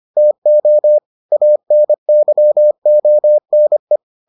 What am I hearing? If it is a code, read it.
Morse code, heard as a keyed tone, about 25 wpm, 600 Hz. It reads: TO ANYONE